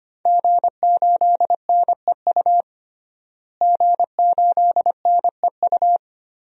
Morse code, heard as a keyed tone, about 25 wpm, 700 Hz. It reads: G8NEU G8NEU
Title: DXCC-100 Call Signs - 5 Characters - Repeated 2x - 25wpm